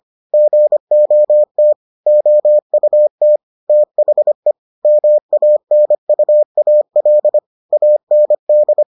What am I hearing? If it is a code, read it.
Morse code, heard as a keyed tone, about 25 wpm, 600 Hz. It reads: GOT OUT THE MANUAL AND